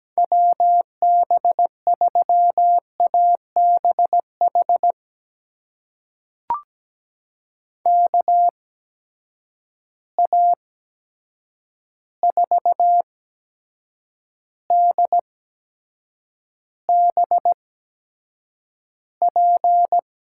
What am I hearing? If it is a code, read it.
Morse code, heard as a keyed tone, about 17 wpm, spaced as for 7 wpm, 700 Hz. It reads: WB3ABH KA4DBP
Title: USA Call Signs - 6 Characters - 8x Character Spacing - 17wpm